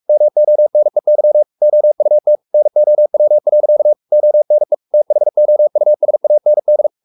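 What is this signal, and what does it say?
Morse code, heard as a keyed tone, about 44 wpm, 600 Hz. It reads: MONEY OUT NOW. ONE THOUSAND
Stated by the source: Morse code at 43 wpm sending Ken Follett's novel Never in English, from CW med Gnister.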